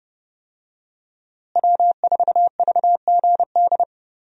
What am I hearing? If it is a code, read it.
Morse code, heard as a keyed tone, about 30 wpm, 700 Hz. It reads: W4VGD
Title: USA Call Signs - 5 Characters - Repeated 3x - 30wpm